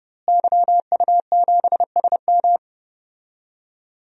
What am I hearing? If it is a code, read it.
Morse code, heard as a keyed tone, about 32 wpm, 700 Hz. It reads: YU7SM
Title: DXCC-100 Call Signs - 5 Characters - Repeated 3x - 30wpm